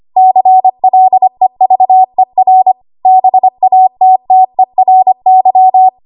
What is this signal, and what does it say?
Morse code, heard as a keyed tone, about 25 wpm, 750 Hz. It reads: CLEVER BATTERY
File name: Archive / 25 WPM / 25WPM